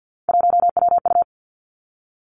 Morse code, heard as a keyed tone, about 50 wpm, 700 Hz. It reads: 1WU
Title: Three Challenging Letter-Number Combinations 50wpm